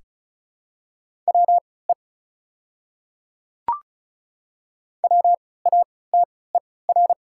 Morse code, heard as a keyed tone, about 34 wpm, spaced as for 20 wpm, 700 Hz. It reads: WE WATER
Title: 3x Character Spacing - Top 100 Words - 35wpm